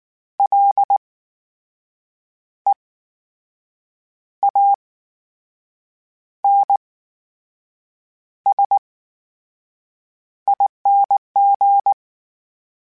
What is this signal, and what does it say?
Morse code, heard as a keyed tone, about 19 wpm, spaced as for 6 wpm, 800 Hz. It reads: LEANSING
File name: Special INGs / Special INGs-mp3s / VerySlowToFaster Ing @18